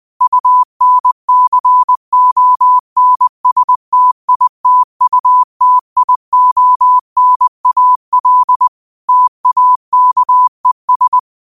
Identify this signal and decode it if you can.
Morse code, heard as a keyed tone, about 20 wpm, 1000 Hz. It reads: UNCONSTITUTIONAL TAKES